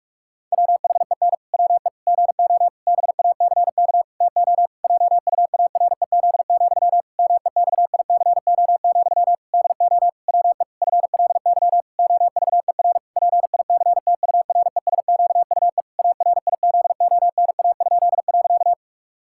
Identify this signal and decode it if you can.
Morse code, heard as a keyed tone, about 46 wpm, 700 Hz. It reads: WHEN WE GO BACK TO JUAREZ, MEXICO, DO WE FLY OVER PICTURESQUE ARIZONA?.
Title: Pangram Challenge v2 - 45wpm